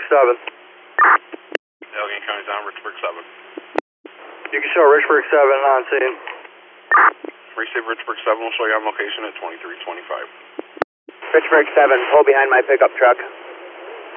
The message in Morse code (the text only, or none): none